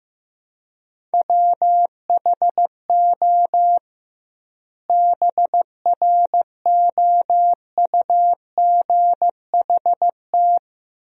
Morse code, heard as a keyed tone, about 15 wpm, 700 Hz. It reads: WHO BROUGHT
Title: 2x Word-Spacing - Sentences from Top 500 Words - 15wpm